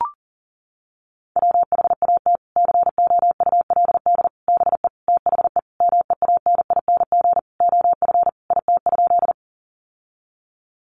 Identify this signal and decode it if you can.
Morse code, heard as a keyed tone, about 40 wpm, 700 Hz. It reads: WHAT COULD BE THE MEANING OF IT?
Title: Sentences from Top 900 Words 40wpm